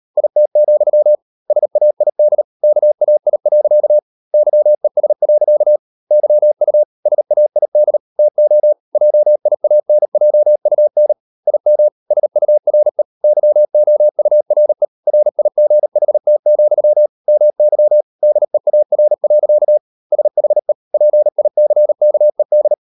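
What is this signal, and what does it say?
Morse code, heard as a keyed tone, about 38 wpm, 600 Hz. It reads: IT, SAID KAI. YES. YU SAID TO JIANJUN IM SURE YOURE RIGHT, MY DEAR. SHE PICKED
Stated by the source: Morse code practice at 35–39 wpm sending English text of Ken Follett's novel Never, from CW med Gnister.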